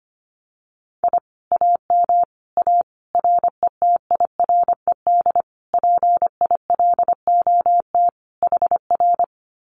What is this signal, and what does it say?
Morse code, heard as a keyed tone, about 25 wpm, 700 Hz. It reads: I AM A RETIRED PILOT HR
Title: Common QSO Phrases Encore 25wpm